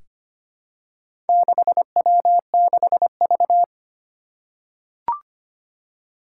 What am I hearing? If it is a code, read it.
Morse code, heard as a keyed tone, about 25 wpm, 700 Hz. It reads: 6W6V
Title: Four Challenging Letter-Number Combinations 25wpm